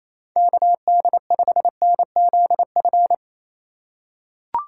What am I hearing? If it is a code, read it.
Morse code, heard as a keyed tone, about 28 wpm, 700 Hz. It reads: KD5NZF
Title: USA Call Signs - 6 Characters - 3x Character Spacing - 28wpm